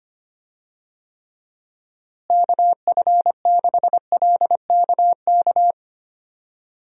Morse code, heard as a keyed tone, about 25 wpm, 700 Hz. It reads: KF6LKK
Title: USA Call Signs - 6 Characters - Repeated 2x - 25wpm